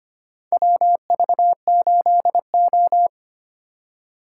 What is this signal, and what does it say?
Morse code, heard as a keyed tone, about 25 wpm, 700 Hz. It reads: WV8O
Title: USA Call Signs - 4 Characters - 2x Character Spacing - 25wpm